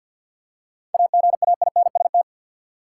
Morse code, heard as a keyed tone, about 50 wpm, 700 Hz. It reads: AGAINST
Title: Seven Letter Words - 50wpm